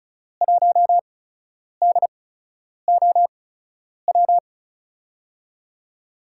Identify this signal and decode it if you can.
Morse code, heard as a keyed tone, about 36 wpm, spaced as for 10 wpm, 700 Hz. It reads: 1DOW
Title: USA Call Signs - 8x Character Spacing - 35wpm